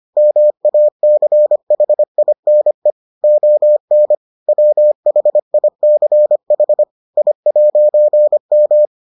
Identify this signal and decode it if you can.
Morse code, heard as a keyed tone, about 25 wpm, 600 Hz. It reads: MACHINE ON WHICH I'M